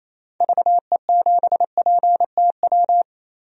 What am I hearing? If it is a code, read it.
Morse code, heard as a keyed tone, about 28 wpm, 700 Hz. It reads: VE7PTW
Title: DXCC-100 Call Signs - 6 Characters - 28wpm